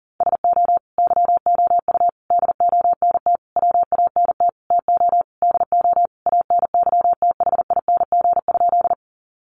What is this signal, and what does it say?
Morse code, heard as a keyed tone, about 40 wpm, 700 Hz. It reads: SO YOU DONT WANT TO DO ANYTHING?
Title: Sentences from Top 1000 Words 40wpm